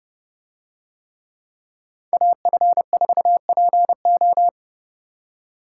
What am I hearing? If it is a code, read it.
Morse code, heard as a keyed tone, about 30 wpm, 700 Hz. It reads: AF4PO